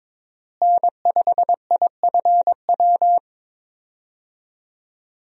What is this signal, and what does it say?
Morse code, heard as a keyed tone, about 22 wpm, 700 Hz. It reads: N5IFW
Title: USA Call Signs - 5 Characters - Repeated 2x - 22wpm